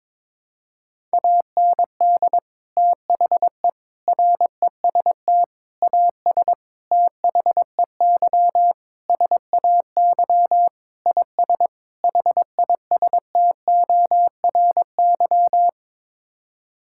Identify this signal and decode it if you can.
Morse code, heard as a keyed tone, about 22 wpm, 700 Hz. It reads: AND THE REST AS THEY SAY IS HISTORY
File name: practice / 22wpm / Sets of 8 Words - 22wpm